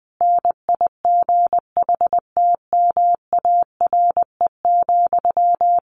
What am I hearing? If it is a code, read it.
Morse code, heard as a keyed tone, about 20 wpm, 700 Hz. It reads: NIGHTMARE,
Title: A Sound Of Thunder 20wpm